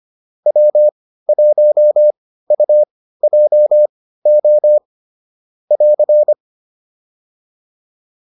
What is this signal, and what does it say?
Morse code, heard as a keyed tone, about 25 wpm, spaced as for 15 wpm, 600 Hz. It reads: W1UJO <AR>
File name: QSO401 15